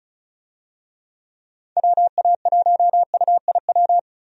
Morse code, heard as a keyed tone, about 36 wpm, 700 Hz. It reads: WA1UIW
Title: USA Call Signs - 6 Characters - Repeated 3x - 35wpm